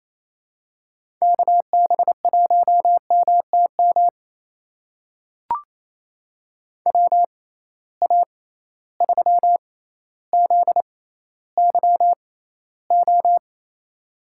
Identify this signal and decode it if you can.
Morse code, heard as a keyed tone, about 27 wpm, spaced as for 14 wpm, 700 Hz. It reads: KB1MTM WA3ZYO
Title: USA Call Signs - 6x Character Spacing - 28wpm